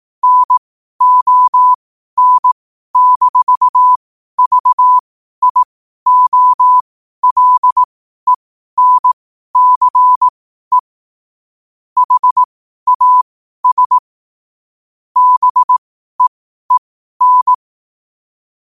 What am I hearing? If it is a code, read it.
Morse code, heard as a keyed tone, about 18 wpm, spaced as for 12 wpm, 1000 Hz. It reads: NON-VIOLENCE HAS BEEN
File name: qotdfiles / 12wpm